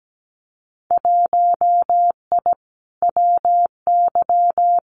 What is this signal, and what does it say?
Morse code, heard as a keyed tone, about 17 wpm, 700 Hz. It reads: 1I WY
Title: Winter Field Day Exchanges Encore 17wpm